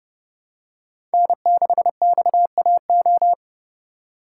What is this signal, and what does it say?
Morse code, heard as a keyed tone, about 30 wpm, 700 Hz. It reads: N6XAO
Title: USA Call Signs - 5 Characters - 3x Character Spacing - 30wpm